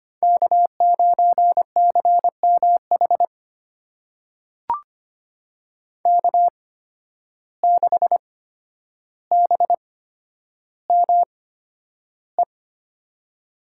Morse code, heard as a keyed tone, about 25 wpm, spaced as for 11 wpm, 700 Hz. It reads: K9CMH K6BME
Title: USA Call Signs - 8x Character Spacing - 25wpm